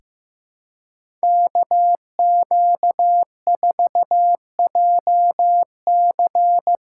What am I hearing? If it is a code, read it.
Morse code, heard as a keyed tone, about 15 wpm, 700 Hz. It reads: KQ4JC
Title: USA Call Signs - 5 Characters - 2x Character Spacing - 15wpm